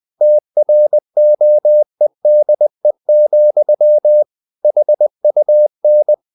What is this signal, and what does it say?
Morse code, heard as a keyed tone, about 20 wpm, 600 Hz. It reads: TROEDE, HUN
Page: Ferie 20 wpm